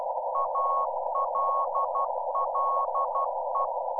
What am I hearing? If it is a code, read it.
Morse code, heard as a keyed tone, about 12 wpm, 1100 Hz. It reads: ALLE
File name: Z PI4VRZ / PI4VRZ 14-jun-2025